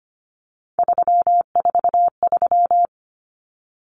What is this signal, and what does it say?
Morse code, heard as a keyed tone, about 25 wpm, 700 Hz. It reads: 343
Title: Common Cut Numbers - 3-Digit 25wpm